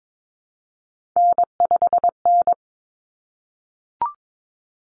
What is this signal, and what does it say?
Morse code, heard as a keyed tone, about 22 wpm, 700 Hz. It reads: N5N
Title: Common Cut Numbers - 3-Digit 22wpm